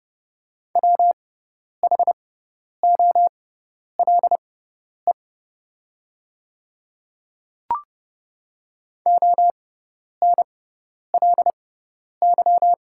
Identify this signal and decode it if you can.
Morse code, heard as a keyed tone, about 30 wpm, spaced as for 10 wpm, 690 Hz. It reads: WHOLE ONLY
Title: 6x Character Spacing - Top 500 Words - 30wpm